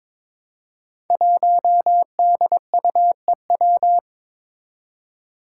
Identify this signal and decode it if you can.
Morse code, heard as a keyed tone, about 22 wpm, 700 Hz. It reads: 1DUEW
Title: Five Challenging Letter-Number Combinations 22wpm